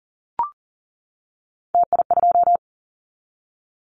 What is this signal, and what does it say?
Morse code, heard as a keyed tone, about 37 wpm, 700 Hz. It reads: TI2
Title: Callsign Prefixes for Top 100 DXCC - 40wpm